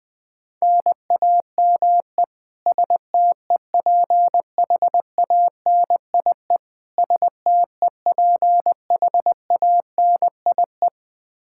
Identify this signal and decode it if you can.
Morse code, heard as a keyed tone, about 20 wpm, 700 Hz. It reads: NAME STEPHANIE STEPHANIE
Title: Practice QSO - Exchange 1 - Beginning - 20wpm